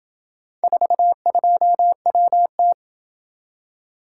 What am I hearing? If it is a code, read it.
Morse code, heard as a keyed tone, about 28 wpm, 700 Hz. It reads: 42WT